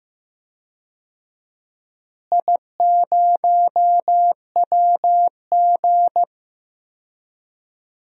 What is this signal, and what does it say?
Morse code, heard as a keyed tone, about 15 wpm, 700 Hz. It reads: I0WG
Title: DXCC-100 Call Signs - 4 Characters - Repeated 2x - 15wpm